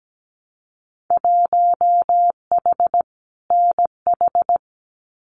Morse code, heard as a keyed tone, about 17 wpm, 700 Hz. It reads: 1H NH
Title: Winter Field Day Exchanges 17wpm